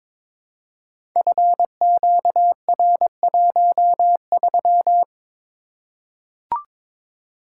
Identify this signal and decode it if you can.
Morse code, heard as a keyed tone, about 22 wpm, 700 Hz. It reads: FQR13
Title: Code Groups - 5 Letter:Numbers - 22wpm